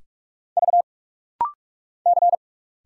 Morse code, heard as a keyed tone, about 44 wpm, 700 Hz. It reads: V C
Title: Single Letters - Rapid-Fire 45wpm